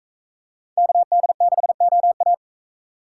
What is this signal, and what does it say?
Morse code, heard as a keyed tone, about 42 wpm, 700 Hz. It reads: KD6OA